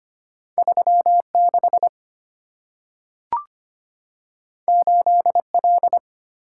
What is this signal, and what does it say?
Morse code, heard as a keyed tone, about 25 wpm, 690 Hz. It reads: 36 8L
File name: Two Letter-Number Combinations - 25wpm